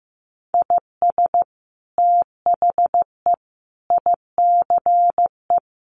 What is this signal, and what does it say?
Morse code, heard as a keyed tone, about 15 wpm, 700 Hz. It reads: IS THE ICE
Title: Sentences from Top 1500 Words 15wpm